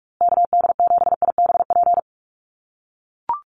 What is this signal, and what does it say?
Morse code, heard as a keyed tone, about 44 wpm, 700 Hz. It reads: KD7IBP
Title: USA Call Signs Repeated 3x 45wpm